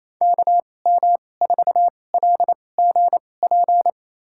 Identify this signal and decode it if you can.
Morse code, heard as a keyed tone, about 28 wpm, spaced as for 20 wpm, 700 Hz. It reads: KM4LGP
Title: USA Call Signs - 2x Character Spacing - 28wpm